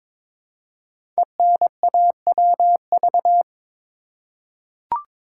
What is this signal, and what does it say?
Morse code, heard as a keyed tone, about 22 wpm, 700 Hz. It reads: ENAWV
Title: Code Groups - 5 Letters - 22wpm